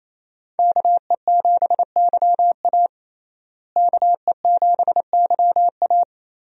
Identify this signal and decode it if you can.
Morse code, heard as a keyed tone, about 28 wpm, 700 Hz. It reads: KE7YA KE7YA
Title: USA Call Signs - 5 Characters - Repeated 3x - 28wpm